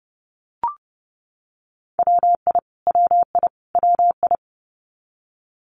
Morse code, heard as a keyed tone, about 30 wpm, 700 Hz. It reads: WI WI WI
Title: US States Repeated 3x 30wpm